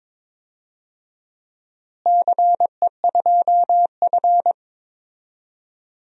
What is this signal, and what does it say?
Morse code, heard as a keyed tone, about 22 wpm, 700 Hz. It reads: CE2F
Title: DXCC-100 Call Signs - 4 Characters - Repeated 3x - 22wpm